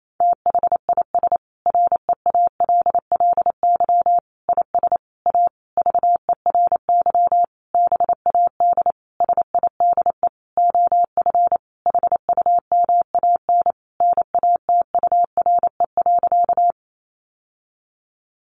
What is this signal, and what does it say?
Morse code, heard as a keyed tone, about 28 wpm, 700 Hz. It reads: THIS REALLY IS A VERY BAD SIDE OF HUMAN NATURE.
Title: Sentences from Top 700 Words 28wpm